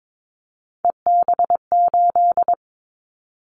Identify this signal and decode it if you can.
Morse code, heard as a keyed tone, about 22 wpm, 700 Hz. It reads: EB8
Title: Callsign Prefixes for Top 50 DXCC - 22wpm